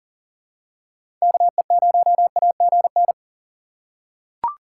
Morse code, heard as a keyed tone, about 40 wpm, 700 Hz. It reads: KE0AGN
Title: USA Call Signs - 6 Characters - 4x Character Spacing - 40wpm